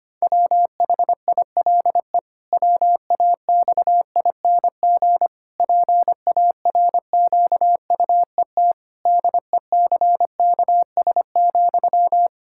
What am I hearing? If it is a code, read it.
Morse code, heard as a keyed tone, about 25 wpm, 700 Hz. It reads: WHILE WAXING PARQUET DECKS,
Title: Pangram Challenge v2 - 25wpm